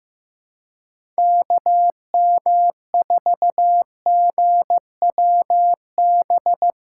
Morse code, heard as a keyed tone, about 15 wpm, 700 Hz. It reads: KM4GWB
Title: USA Call Signs - 6 Characters - 2x Character Spacing - 15wpm